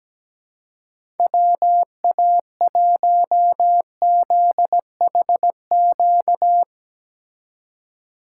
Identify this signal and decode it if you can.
Morse code, heard as a keyed tone, about 17 wpm, 700 Hz. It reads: WA1ZHQ
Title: USA Call Signs - 6 Characters - 2x Character Spacing - 17wpm